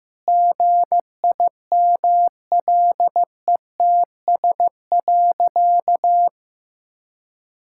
Morse code, heard as a keyed tone, about 15 wpm, 700 Hz. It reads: GIMLETS.
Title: Pangram Challenge v2 - 15wpm